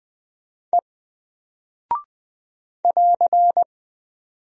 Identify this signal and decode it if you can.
Morse code, heard as a keyed tone, about 20 wpm, 700 Hz. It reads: E <AR>
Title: Single Character (Letter:Number:Punctuation:Prosign) Alt 20wpm